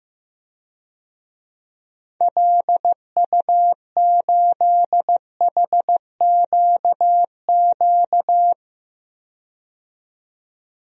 Morse code, heard as a keyed tone, about 16 wpm, 700 Hz. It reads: LU8HQQ
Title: DXCC-100 Call Signs - 6 Characters - 15wpm